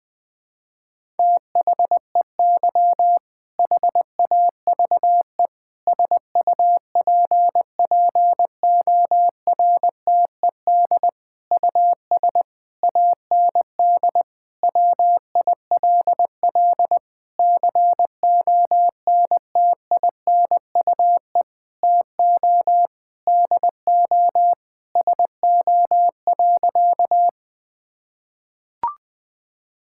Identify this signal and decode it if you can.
Morse code, heard as a keyed tone, about 20 wpm, 700 Hz. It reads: THEY HAVE SUPPORTED US AND WILL CONTINUE TO DO SO.
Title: Sentences with 6-Word Phrases - 20wpm